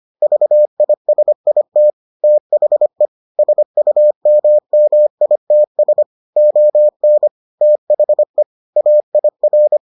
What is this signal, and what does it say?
Morse code, heard as a keyed tone, about 25 wpm, 600 Hz. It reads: VISIT THE SUMMITS ON THE AIR